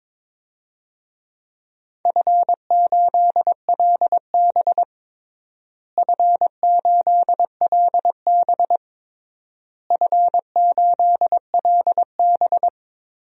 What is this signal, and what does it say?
Morse code, heard as a keyed tone, about 22 wpm, 700 Hz. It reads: F8LB F8LB F8LB
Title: DXCC-100 Call Signs - 4 Characters - Repeated 3x - 22wpm